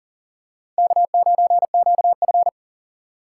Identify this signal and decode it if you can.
Morse code, heard as a keyed tone, about 40 wpm, 700 Hz. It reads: K9QF
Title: USA Call Signs - 4 Characters - 2x Character Spacing - 40wpm